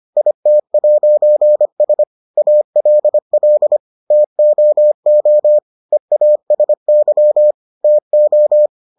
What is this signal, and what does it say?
Morse code, heard as a keyed tone, about 25 wpm, 600 Hz. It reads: IT'S ALL TOO EASY TO